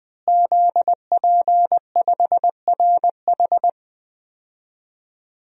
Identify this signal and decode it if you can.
Morse code, heard as a keyed tone, about 20 wpm, 700 Hz. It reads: ZP5RH
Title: DXCC-100 Call Signs - 5 Characters - Repeated 3x - 20wpm